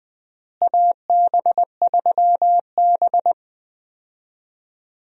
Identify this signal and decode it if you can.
Morse code, heard as a keyed tone, about 20 wpm, 700 Hz. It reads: AB3B
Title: USA Call Signs - 4 Characters - Repeated 3x - 20wpm